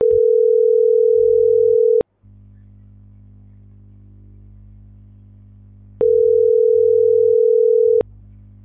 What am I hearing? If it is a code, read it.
Ringback tone (North America), 440 + 480 Hz.